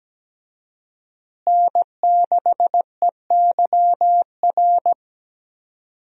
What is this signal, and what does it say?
Morse code, heard as a keyed tone, about 17 wpm, 700 Hz. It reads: N6EYR
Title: USA Call Signs - 5 Characters - Repeated 3x - 17wpm